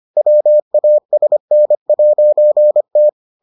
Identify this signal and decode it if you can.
Morse code, heard as a keyed tone, about 25 wpm, 600 Hz. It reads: WASN'T